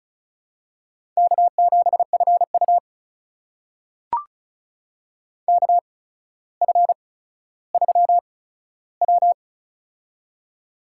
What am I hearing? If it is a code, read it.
Morse code, heard as a keyed tone, about 36 wpm, spaced as for 15 wpm, 700 Hz. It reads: K7FU KF3W
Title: USA Call Signs - 4 Characters - 8x Character Spacing - 35wpm